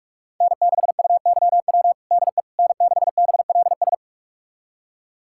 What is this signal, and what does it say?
Morse code, heard as a keyed tone, about 41 wpm, 700 Hz. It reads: N6UYW DE N6BLS
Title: USA Callsign DE Callsign - 5 Characters - 45wpm